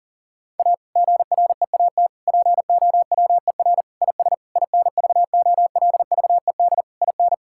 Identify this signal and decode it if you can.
Morse code, heard as a keyed tone, about 40 wpm, 700 Hz. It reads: A GREAT POWER IS INVOLVED IN
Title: Sentences from Top 1000 Words - Review Part 5 - 40wpm